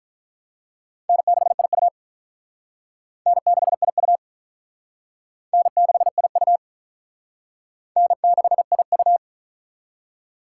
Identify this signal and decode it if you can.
Morse code, heard as a keyed tone, about 45 wpm, 700 Hz. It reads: N6IU N6IU N6IU N6IU